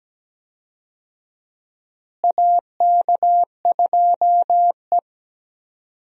Morse code, heard as a keyed tone, about 17 wpm, 700 Hz. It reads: AK2E
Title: USA Call Signs - 4 Characters - Repeated 2x - 17wpm